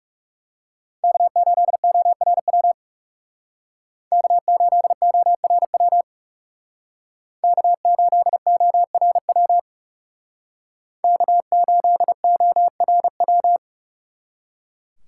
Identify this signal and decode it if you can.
Morse code, heard as a keyed tone, about 37 wpm, 700 Hz. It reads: K8ORW K8ORW K8ORW K8ORW
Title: Speed Racing - USA 1x3 Call Signs 30wpm